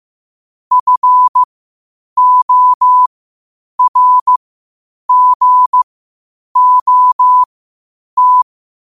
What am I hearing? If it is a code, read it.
Morse code, heard as a keyed tone, about 15 wpm, spaced as for 8 wpm, 1000 Hz. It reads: FORGOT
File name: 8wpm